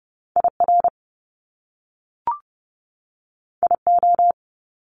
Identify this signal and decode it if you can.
Morse code, heard as a keyed tone, about 30 wpm, 700 Hz. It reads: IR IO
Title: Top Two Letter Combinations in Words 30wpm